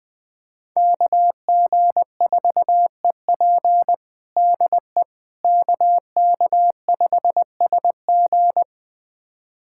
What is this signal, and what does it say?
Morse code, heard as a keyed tone, about 21 wpm, 700 Hz. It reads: KG4EP DE KK5SG
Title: USA Callsign DE Callsign - 5 Characters - 20wpm